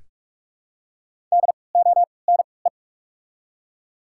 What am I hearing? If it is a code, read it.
Morse code, heard as a keyed tone, about 45 wpm, 700 Hz. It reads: D O N E
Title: 3x Character Spacing - Top 500 Words - 45wpm